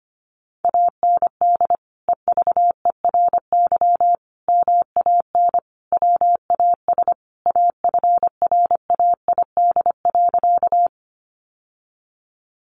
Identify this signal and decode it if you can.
Morse code, heard as a keyed tone, about 25 wpm, 700 Hz. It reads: AND EVERY MAN WAS AFRAID.
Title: Sentences from Top 1500 Words 25wpm